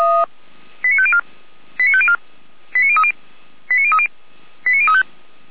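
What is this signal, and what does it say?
Five-tone selective call (ZVEI), heard as quick runs of five tones, tones of about 70 ms. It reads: 89504 89504 89030 89030 89035
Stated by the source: ZVEI five-tone call received by SDR, from feuerwehr-orlamuende (fire brigade).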